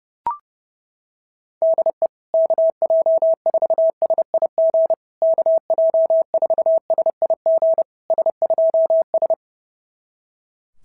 Morse code, heard as a keyed tone, about 28 wpm, 650 Hz. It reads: DE KJ4SIG KJ4SIG S2S
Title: SOTA QSO Practice 30wpm